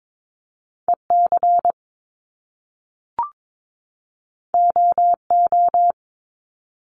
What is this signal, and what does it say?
Morse code, heard as a keyed tone, about 22 wpm, 700 Hz. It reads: EC OO
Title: Top Two Letter Combinations in Words 22wpm